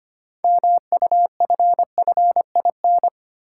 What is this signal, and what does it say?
Morse code, heard as a keyed tone, about 25 wpm, 700 Hz. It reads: MUFFIN
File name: practice / 25wpm / Easy Words with Hints - 25wpm